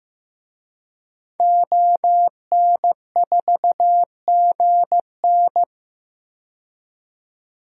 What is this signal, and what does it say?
Morse code, heard as a keyed tone, about 15 wpm, 700 Hz. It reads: ON4GN
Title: DXCC-100 Call Signs - 5 Characters - 15wpm